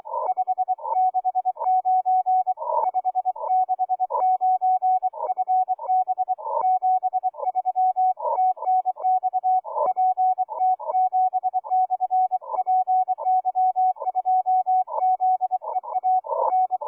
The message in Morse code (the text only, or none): 569 569 FB 73 TNX PT7/PY2ZEA N